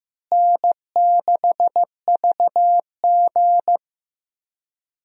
Morse code, heard as a keyed tone, about 15 wpm, 700 Hz. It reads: N6VG
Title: USA Call Signs - 4 Characters - Repeated 3x - 15wpm